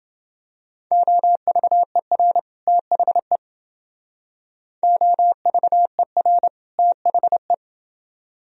Morse code, heard as a keyed tone, about 28 wpm, 700 Hz. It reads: OVER THE OVER THE